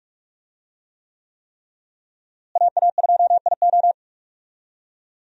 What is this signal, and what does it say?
Morse code, heard as a keyed tone, about 42 wpm, 700 Hz. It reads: AA2IO
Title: USA Call Signs - 5 Characters - Repeated 3x - 45wpm